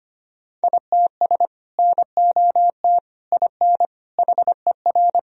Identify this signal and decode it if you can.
Morse code, heard as a keyed tone, about 25 wpm, 700 Hz. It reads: ITS NOT IN HER
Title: Sentences from Top 1000 Words - Review Part 2 - 25wpm